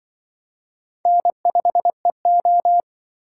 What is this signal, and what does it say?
Morse code, heard as a keyed tone, about 26 wpm, 700 Hz. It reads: N5EO